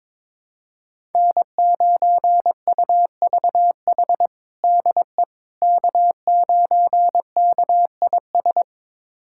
Morse code, heard as a keyed tone, about 21 wpm, 700 Hz. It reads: N9UVH DE K9KIS
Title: USA Callsign DE Callsign - 5 Characters - 22wpm